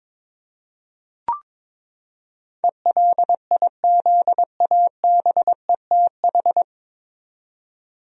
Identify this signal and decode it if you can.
Morse code, heard as a keyed tone, about 22 wpm, 700 Hz. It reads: ELIZABETH
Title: USA Common Names - 22wpm